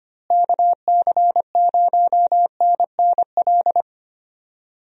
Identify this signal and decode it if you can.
Morse code, heard as a keyed tone, about 25 wpm, 700 Hz. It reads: KC0NNL
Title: USA Call Signs - 6 Characters - Repeated 3x - 25wpm